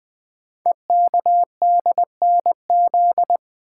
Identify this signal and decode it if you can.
Morse code, heard as a keyed tone, about 20 wpm, 700 Hz. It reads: EKDNZ
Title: Code Groups - 5 Letters - 20wpm